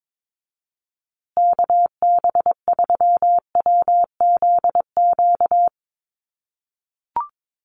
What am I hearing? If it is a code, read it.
Morse code, heard as a keyed tone, about 22 wpm, 700 Hz. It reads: KB3WZQ
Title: USA Call Signs Repeated 3x 22wpm